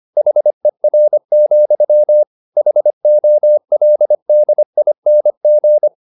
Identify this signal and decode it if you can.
Morse code, heard as a keyed tone, about 25 wpm, 600 Hz. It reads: HER, HOLDING
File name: the old man and the sea part 1 25wpm